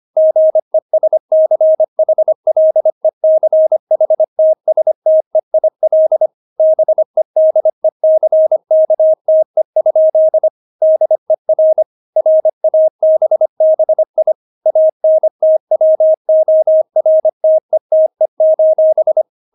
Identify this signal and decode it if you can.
Morse code, heard as a keyed tone, about 25 wpm, 630 Hz. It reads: GESCHLECHTSTEIL BEDECKTE? DER RABBI ANTWORTETE:
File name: Witz6 25wpm